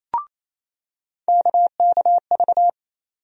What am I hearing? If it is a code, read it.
Morse code, heard as a keyed tone, about 28 wpm, 700 Hz. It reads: KKV
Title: USA Call Sign Postfixes - 3 Letters 28wpm